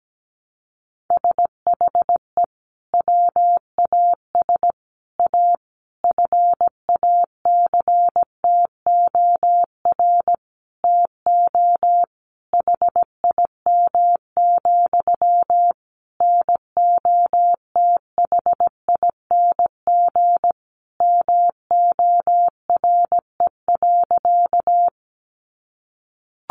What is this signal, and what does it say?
Morse code, heard as a keyed tone, about 17 wpm, 700 Hz. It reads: SHE WAS A FACTOR TO HIM, NOTHING MORE.
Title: Sentences from Top 1300 Words 17wpm